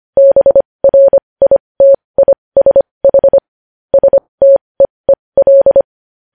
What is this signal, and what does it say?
Morse code, heard as a keyed tone, about 25 wpm, spaced as for 20 wpm, 550 Hz. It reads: BRITISH STEEL
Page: news headlines in morse code at 20 wpm